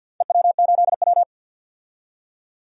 Morse code, heard as a keyed tone, about 47 wpm, 700 Hz. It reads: EW8W
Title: DXCC-100 Call Signs - 4 Characters - 50wpm